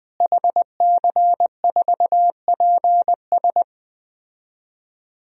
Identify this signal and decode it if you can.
Morse code, heard as a keyed tone, about 20 wpm, 700 Hz. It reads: HC4PS
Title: DXCC-100 Call Signs - 5 Characters - Repeated 3x - 20wpm